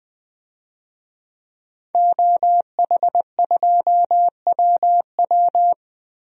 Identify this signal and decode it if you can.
Morse code, heard as a keyed tone, about 20 wpm, 700 Hz. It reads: OH2WW
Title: DXCC-100 Call Signs - 5 Characters - Repeated 3x - 20wpm